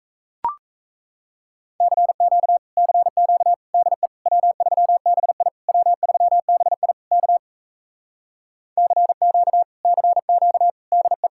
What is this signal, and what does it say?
Morse code, heard as a keyed tone, about 37 wpm, 700 Hz. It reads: CQ CQ DE W3BI W3BI K CQ CQ DE